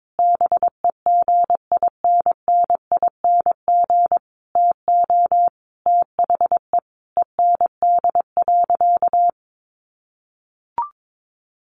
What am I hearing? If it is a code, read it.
Morse code, heard as a keyed tone, about 22 wpm, 700 Hz. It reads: BEGINNING TO THE END.
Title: Sentences from Top 1300 Words 22wpm